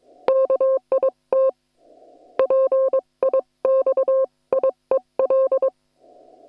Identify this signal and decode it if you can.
Morse code, heard as a keyed tone, about 22 wpm, 530 Hz. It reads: KIT PIXIEL
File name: attachments / hf 1 001